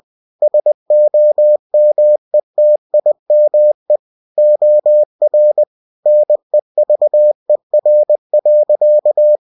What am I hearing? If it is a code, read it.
Morse code, heard as a keyed tone, about 20 wpm, 600 Hz. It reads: SOMETIME OR NEVER.
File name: cw-books / winnie-the-pooh / winnie-20wpm-page-0078